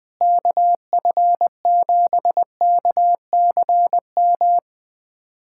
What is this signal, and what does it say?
Morse code, heard as a keyed tone, about 21 wpm, 700 Hz. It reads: KF7KCM
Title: USA Callsign DE Callsign - 6 Characters - 20wpm